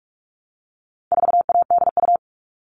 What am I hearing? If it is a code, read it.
Morse code, heard as a keyed tone, about 44 wpm, 700 Hz. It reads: 4ADU
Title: All Cut Numbers - 4-Digit 45wpm